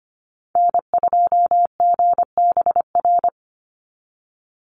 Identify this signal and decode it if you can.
Morse code, heard as a keyed tone, about 25 wpm, 700 Hz. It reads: N2GBR
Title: NA SOTA Chaser Callsigns 25wpm